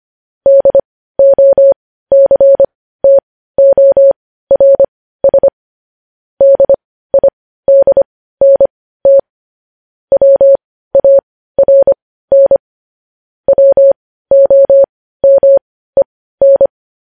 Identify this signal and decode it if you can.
Morse code, heard as a keyed tone, about 25 wpm, spaced as for 15 wpm, 550 Hz. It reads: DOCTORS DIDNT WARN WOMEN
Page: News Headlines in Morse Code at 15 WPM